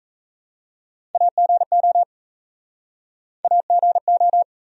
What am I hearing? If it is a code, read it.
Morse code, heard as a keyed tone, about 37 wpm, 700 Hz. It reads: AGO AGO